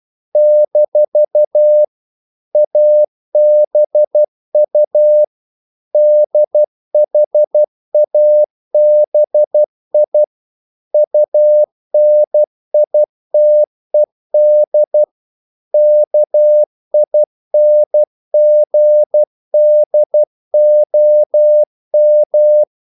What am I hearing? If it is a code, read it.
Morse code, heard as a keyed tone, about 12 wpm, 600 Hz. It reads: - ABU DHABI UNITED KINGDOM
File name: CW / Practice / NationalCapitals12